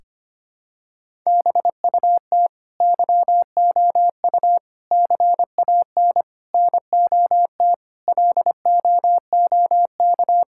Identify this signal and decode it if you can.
Morse code, heard as a keyed tone, about 25 wpm, 700 Hz. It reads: BUT YOU CAN NOT LOOK
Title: Sentences from Top 1000 Words - Review Part 6 - 25wpm